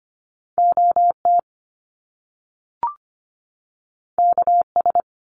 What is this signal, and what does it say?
Morse code, heard as a keyed tone, about 25 wpm, 700 Hz. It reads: OT KS
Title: Top Two Letter Combinations in Words 25wpm